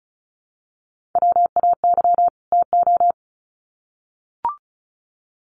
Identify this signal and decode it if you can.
Morse code, heard as a keyed tone, about 34 wpm, 700 Hz. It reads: WAY TO
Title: Sets of 2 Words 35wpm